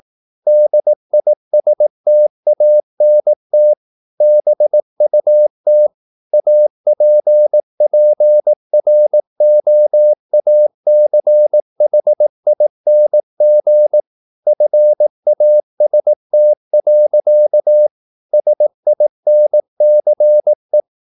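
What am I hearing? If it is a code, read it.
Morse code, heard as a keyed tone, about 18 wpm, 600 Hz. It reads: DISTANT BUT APPROACHING FAST. SINCE
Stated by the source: Morse practice at 18 wpm, English text, from CW med Gnister.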